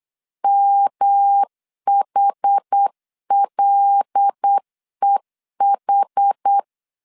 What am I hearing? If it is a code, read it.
Morse code, heard as a keyed tone, about 8 wpm, 800 Hz. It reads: MHLEH
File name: LEZIONI / 6 PKX0 / 6 PK 8wpm 3003